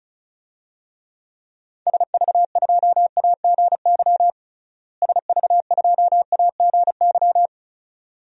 Morse code, heard as a keyed tone, about 34 wpm, 700 Hz. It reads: SV2AGY SV2AGY
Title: DXCC-100 Call Signs - 6 Characters - Repeated 2x - 35wpm